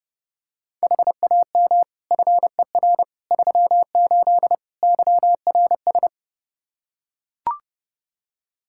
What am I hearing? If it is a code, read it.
Morse code, heard as a keyed tone, about 30 wpm, 700 Hz. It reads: HAM FER 38 YRS
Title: K1USN SST QSO Practice 30wpm